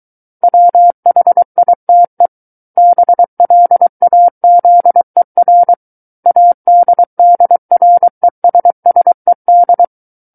Morse code, heard as a keyed tone, about 23 wpm, 700 Hz. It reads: WHITE BLAZER ADDRESSED
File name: Book One / 14-Chapter 14 - 23 WPM